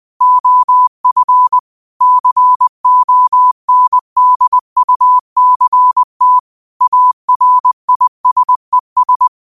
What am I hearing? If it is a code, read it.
Morse code, heard as a keyed tone, about 20 wpm, 1000 Hz. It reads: OF CONDUCT ARISES
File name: qotdfiles / 20wpm